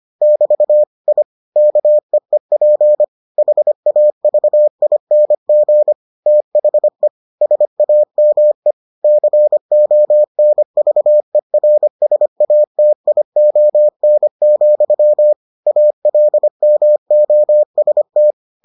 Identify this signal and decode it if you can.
Morse code, heard as a keyed tone, about 25 wpm, 600 Hz. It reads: <BT> I KEEP HAVING THE SAME CONVERSATION, ALMOST